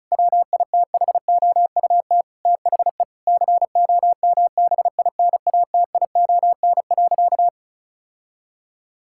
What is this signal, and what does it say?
Morse code, heard as a keyed tone, about 34 wpm, 700 Hz. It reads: WITHOUT THE COMBINATION.
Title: Sentences from Top 1000 Words - Review Part 6 - 35wpm